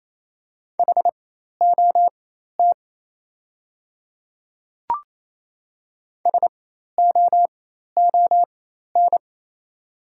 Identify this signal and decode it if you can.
Morse code, heard as a keyed tone, about 28 wpm, spaced as for 13 wpm, 700 Hz. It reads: HOT SOON
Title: 4x Character Spacing - Top 500 Words - 28wpm